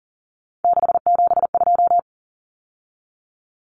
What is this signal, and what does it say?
Morse code, heard as a keyed tone, about 40 wpm, 700 Hz. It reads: 672
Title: Numbers - 3-Digit 40wpm